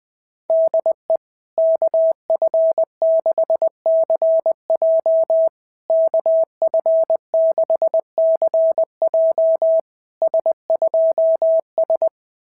Morse code, heard as a keyed tone, about 20 wpm, 650 Hz. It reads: DE KF6CJ KF6CJ S2S
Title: SOTA QSO Practice 20wpm